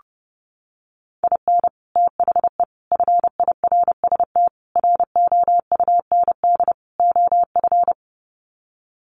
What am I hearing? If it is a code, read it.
Morse code, heard as a keyed tone, about 30 wpm, 700 Hz. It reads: IN THE FIRST ROUND OF
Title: Sets of 5 Words Encore 30wpm